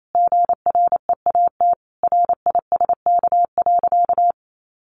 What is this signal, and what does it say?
Morse code, heard as a keyed tone, about 28 wpm, 700 Hz. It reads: GREAT RISK.
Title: Sentences from Top 600 Words 28wpm